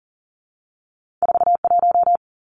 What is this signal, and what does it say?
Morse code, heard as a keyed tone, about 40 wpm, 700 Hz. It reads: 4 1
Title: Two Challenging Letter-Number Combinations 40wpm